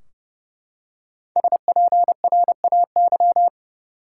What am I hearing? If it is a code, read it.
Morse code, heard as a keyed tone, about 29 wpm, 700 Hz. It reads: SPRAY
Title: Five Letter Words - 30wpm